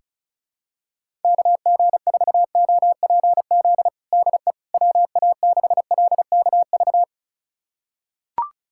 Morse code, heard as a keyed tone, about 36 wpm, 700 Hz. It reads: KG4OPZ DE WA6LKV
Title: USA Callsign DE Callsign - 6 Characters - 35wpm